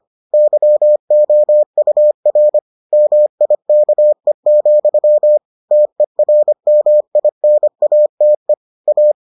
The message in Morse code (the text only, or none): YOUR MIKE, TERMINATE A